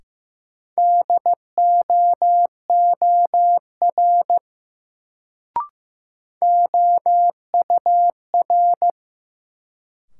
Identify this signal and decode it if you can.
Morse code, heard as a keyed tone, about 15 wpm, 700 Hz. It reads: DOOR OUR
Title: Sets of 2 Words - New Word Drill - 15wpm